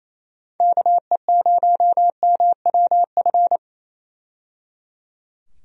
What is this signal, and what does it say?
Morse code, heard as a keyed tone, about 28 wpm, 700 Hz. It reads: KE0MWF
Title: USA Call Signs - 6 Characters - 28wpm